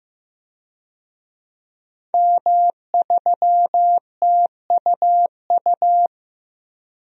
Morse code, heard as a keyed tone, about 16 wpm, 700 Hz. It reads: M3TUU
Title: DXCC-100 Call Signs - 5 Characters - Repeated 2x - 15wpm